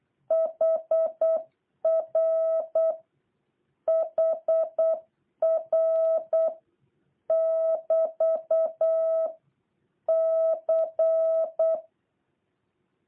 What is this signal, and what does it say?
Morse code, heard as a keyed tone, about 8 wpm, 650 Hz. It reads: HR HR <BT> C